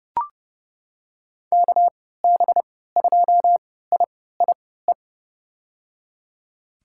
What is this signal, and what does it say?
Morse code, heard as a keyed tone, about 31 wpm, spaced as for 17 wpm, 700 Hz. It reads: KB2IIE
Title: USA Call Signs - 6 Characters - 3x Character Spacing - 30wpm